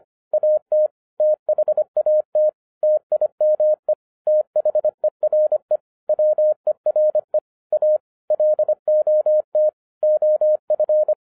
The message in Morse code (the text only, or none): AT THAT TIME THERE WERE A LOT OF